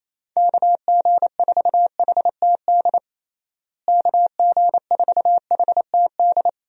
Morse code, heard as a keyed tone, about 28 wpm, 700 Hz. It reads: KG4HTD KG4HTD
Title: USA Call Signs - 6 Characters - Repeated 2x - 28wpm